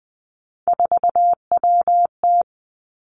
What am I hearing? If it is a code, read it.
Morse code, heard as a keyed tone, about 19 wpm, 700 Hz. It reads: 4WT
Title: Three Challenging Letter-Number Combinations 20wpm